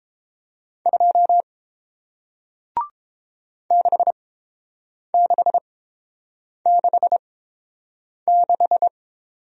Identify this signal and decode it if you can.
Morse code, heard as a keyed tone, about 30 wpm, spaced as for 7 wpm, 700 Hz. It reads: 2 6666